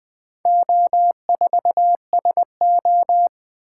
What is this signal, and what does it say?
Morse code, heard as a keyed tone, about 20 wpm, 700 Hz. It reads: O4SO
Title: Four Letter-Number Combinations - 20wpm